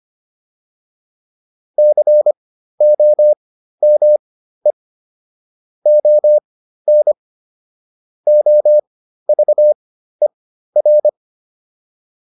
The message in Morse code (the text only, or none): COME ON OVER